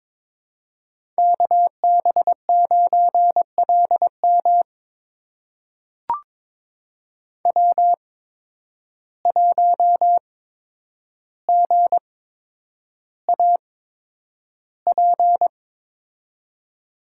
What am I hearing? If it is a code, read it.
Morse code, heard as a keyed tone, about 23 wpm, spaced as for 10 wpm, 700 Hz. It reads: KB9LM W1GAP
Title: USA Call Signs - 5 Characters - 8x Character Spacing - 22wpm